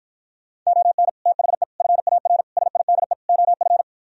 Morse code, heard as a keyed tone, about 56 wpm, 700 Hz. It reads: ON THE FAR SIDE OF